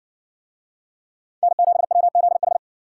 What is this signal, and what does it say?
Morse code, heard as a keyed tone, about 57 wpm, 700 Hz. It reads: N7WZH